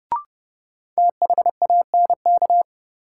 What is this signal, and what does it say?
Morse code, heard as a keyed tone, about 30 wpm, 700 Hz. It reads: THANK
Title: Sets of 2 Words - New Word Drill - 30wpm